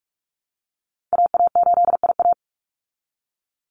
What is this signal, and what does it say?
Morse code, heard as a keyed tone, about 44 wpm, 700 Hz. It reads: AA8IA
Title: Call Signs Repeated 3x - 45wpm